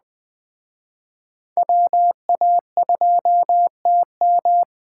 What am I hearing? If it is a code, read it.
Morse code, heard as a keyed tone, about 21 wpm, 700 Hz. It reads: WA2TM
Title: Speed Racing - USA 2x2 Call Signs 17wpm